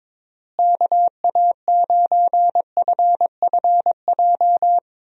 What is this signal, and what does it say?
Morse code, heard as a keyed tone, about 22 wpm, 700 Hz. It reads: KA9FFJ
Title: USA Call Signs - 6 Characters - Repeated 3x - 22wpm